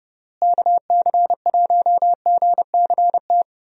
Morse code, heard as a keyed tone, about 30 wpm, 700 Hz. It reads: KC1GCT
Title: USA Call Signs - 6 Characters - Repeated 3x - 30wpm